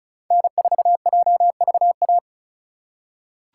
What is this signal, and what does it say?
Morse code, heard as a keyed tone, about 36 wpm, 700 Hz. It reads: N4JVA
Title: USA Call Signs - 3x Character Spacing - 35wpm